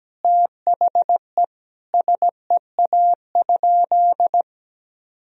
Morse code, heard as a keyed tone, about 17 wpm, 700 Hz. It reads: THE SEA?
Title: Sentences from Top 300 Words and Abbreviations 17wpm